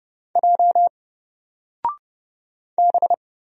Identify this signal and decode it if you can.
Morse code, heard as a keyed tone, about 30 wpm, 700 Hz. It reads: J B
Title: Single Character (Letter:Number:Punctuation:Prosign) 30wpm